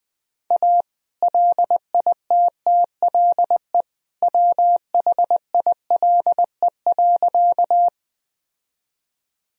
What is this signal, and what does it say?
Morse code, heard as a keyed tone, about 20 wpm, 700 Hz. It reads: A LITTLE WHILE.
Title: Sentences from Top 1000 Words - Review Part 1 - 20wpm